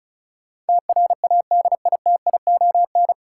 Morse code, heard as a keyed tone, about 34 wpm, 700 Hz. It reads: TRADITION
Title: Sentences from Top 1500 Words - New Word Drill Alt - 35wpm